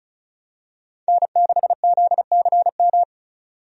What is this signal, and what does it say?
Morse code, heard as a keyed tone, about 36 wpm, 700 Hz. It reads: N6ZCM
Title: USA Call Signs - 5 Characters - 35wpm